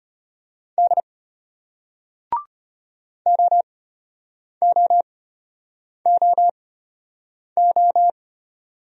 Morse code, heard as a keyed tone, about 32 wpm, spaced as for 8 wpm, 700 Hz. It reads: D OOOO